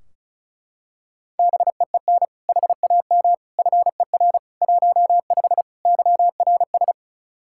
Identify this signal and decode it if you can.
Morse code, heard as a keyed tone, about 36 wpm, 700 Hz. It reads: BEEN HAM FER 15 YRS
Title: K1USN SST QSO Practice 35wpm